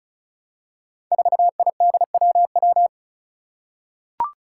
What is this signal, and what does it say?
Morse code, heard as a keyed tone, about 34 wpm, 700 Hz. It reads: 4IDWW
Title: Five Challenging Letter-Number Combinations 35wpm